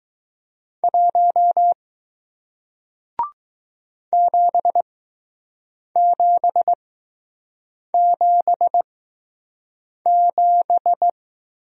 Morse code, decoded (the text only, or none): 1 7777